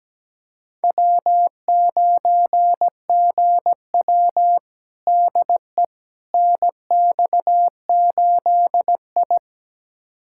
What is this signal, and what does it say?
Morse code, heard as a keyed tone, about 17 wpm, 700 Hz. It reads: W9GW DE NX8I
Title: USA Callsign DE Callsign - 4 Characters - 17wpm